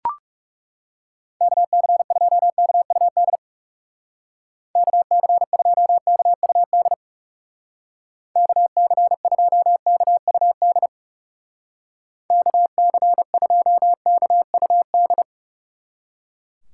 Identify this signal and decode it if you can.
Morse code, heard as a keyed tone, about 39 wpm, 700 Hz. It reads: KC2KUD KC2KUD KC2KUD KC2KUD